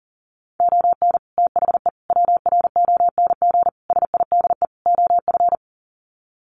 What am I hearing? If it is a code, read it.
Morse code, heard as a keyed tone, about 40 wpm, 700 Hz. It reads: ON THE WRONG SIDE OF
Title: Sets of 5 Words Encore 40wpm